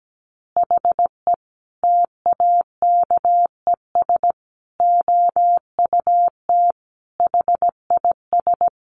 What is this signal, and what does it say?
Morse code, heard as a keyed tone, about 17 wpm, 700 Hz. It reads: HE TAKES OUT HIS
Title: Additional Easy Phrases 17wpm